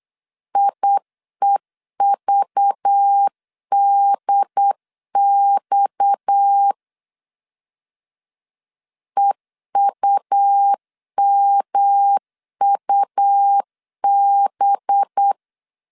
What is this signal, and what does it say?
Morse code, heard as a keyed tone, about 8 wpm, 800 Hz. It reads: IEVDX EUMUB
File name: CW / DET / 1 6 1500C S B 8wpm 5142